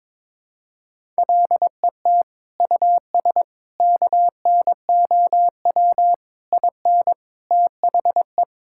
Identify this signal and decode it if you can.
Morse code, heard as a keyed tone, about 22 wpm, 700 Hz. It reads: LET US KNOW IN THE
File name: Sentences with 7-Word Phrases - 22wpm